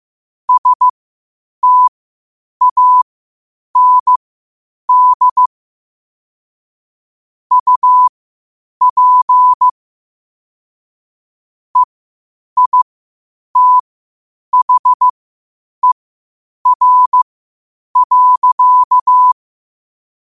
Morse code, heard as a keyed tone, about 15 wpm, spaced as for 8 wpm, 1000 Hz. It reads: STAND UP EITHER.